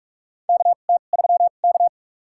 Morse code, heard as a keyed tone, about 47 wpm, spaced as for 33 wpm, 700 Hz. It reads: KT3K
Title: USA Call Signs - 4 Characters - 2x Character Spacing - 45wpm